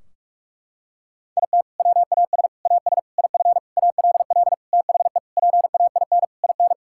Morse code, heard as a keyed tone, about 46 wpm, 700 Hz. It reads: IT WAS AS IF ALL THE PAIN IN
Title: Sentences from Top 1600 Words 45wpm